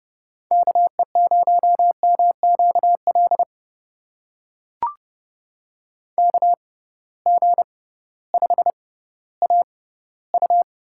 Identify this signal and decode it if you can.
Morse code, heard as a keyed tone, about 32 wpm, spaced as for 17 wpm, 700 Hz. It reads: KE0MQL KG5AU
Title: USA Call Signs - 6x Character Spacing - 30wpm